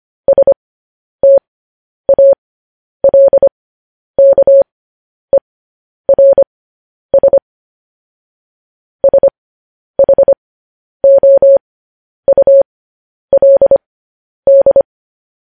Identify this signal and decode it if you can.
Morse code, heard as a keyed tone, about 25 wpm, spaced as for 10 wpm, 550 Hz. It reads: STALKERS SHOULD
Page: news headlines in morse code at 10 wpm